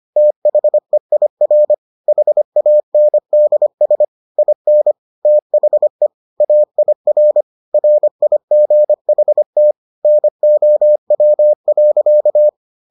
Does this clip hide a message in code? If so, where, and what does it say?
Yes: Morse code from 0.2 to 12.5 s, heard as a keyed tone, about 25 wpm, 600 Hz. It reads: THEIR HANDS IN THE AIR RIGHT NOW.